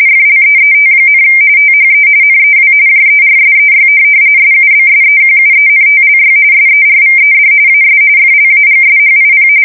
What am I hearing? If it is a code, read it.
RTTY, about 45.45 baud. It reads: FLEET, THE NEW JERSEY SUBSEQUENTLY SERVED AS FLAGSHIP
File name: fsk170 / USS-New-Jersey